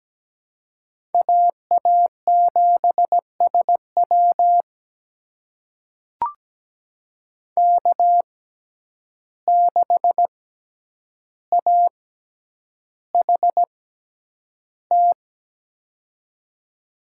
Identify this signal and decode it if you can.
Morse code, heard as a keyed tone, about 17 wpm, spaced as for 9 wpm, 700 Hz. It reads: AA7SW K6AHT
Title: USA Call Signs - 5 Characters - 6x Character Spacing - 17wpm